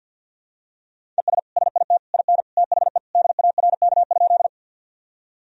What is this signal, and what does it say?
Morse code, heard as a keyed tone, about 53 wpm, 700 Hz. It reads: ES SIT IN THE DARK?
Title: Sentences from Top 500 Words and Abbreviations 50wpm